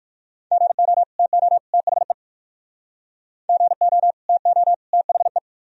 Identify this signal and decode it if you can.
Morse code, heard as a keyed tone, about 58 wpm, 700 Hz. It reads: GO TO THE GO TO THE